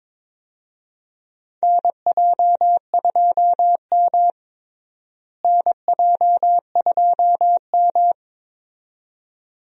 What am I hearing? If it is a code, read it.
Morse code, heard as a keyed tone, about 21 wpm, 700 Hz. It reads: NJ2M NJ2M
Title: USA Call Signs - 4 Characters - Repeated 2x - 22wpm